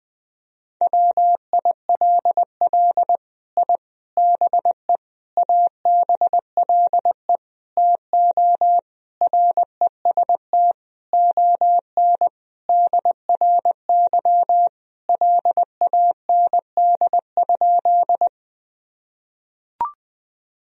Morse code, heard as a keyed tone, about 20 wpm, 700 Hz. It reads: WILL I BE ABLE TO REST ON DRY LAND?
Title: Sentences from Top 1000 Words - Review Part 3 - 20wpm